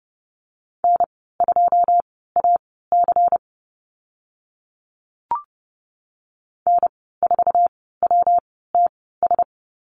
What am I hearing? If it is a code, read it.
Morse code, heard as a keyed tone, about 30 wpm, spaced as for 17 wpm, 700 Hz. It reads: N2AC N4WTS
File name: Lesson 19g - Letter w - Call Signs 30wpm